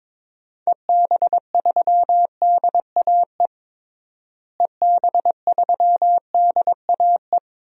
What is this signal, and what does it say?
Morse code, heard as a keyed tone, about 22 wpm, 700 Hz. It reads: EB3DAE EB3DAE
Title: DXCC-100 Call Signs - 6 Characters - Repeated 3x - 22wpm